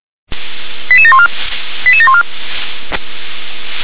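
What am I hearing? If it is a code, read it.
Five-tone selective call (ZVEI), heard as quick runs of five tones, tones of about 70 ms. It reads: 88714 88714
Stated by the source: Received via SDR; ZVEI five-tone selective call 88669.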